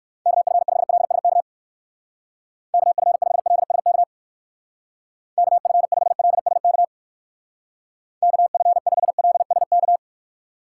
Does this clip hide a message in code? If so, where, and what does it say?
Yes: Morse code from 0.3 to 10.0 s, heard as a keyed tone, about 59 wpm, 700 Hz. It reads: KF5LSK KF5LSK KF5LSK KF5LSK